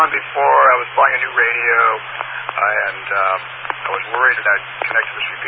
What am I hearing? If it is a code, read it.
Emergency Alert System (SAME) tones, heard as harsh buzzing bursts, with speech. It is heard from 1.1 to 4.7 s.